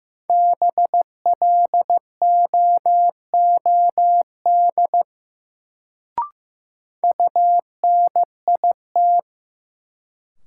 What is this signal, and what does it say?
Morse code, heard as a keyed tone, about 15 wpm, 700 Hz. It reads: BLOOD UNIT
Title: Sentences from Top 1000 Words - Review Part 2 - New Word Drill - 15wpm